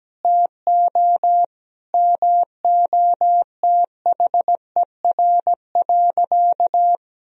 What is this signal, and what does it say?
Morse code, heard as a keyed tone, about 17 wpm, 700 Hz. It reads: TO MOTHER.
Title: Sentences from Top 1000 Words - Review Part 4 - 17wpm